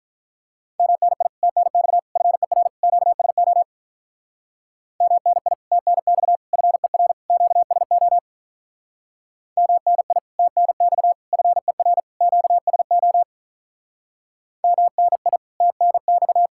Morse code, heard as a keyed tone, about 46 wpm, 700 Hz. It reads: MNI TNX FER QSO MNI TNX FER QSO MNI TNX FER QSO MNI TNX